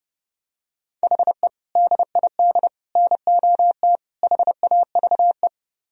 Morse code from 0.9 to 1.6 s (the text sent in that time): HE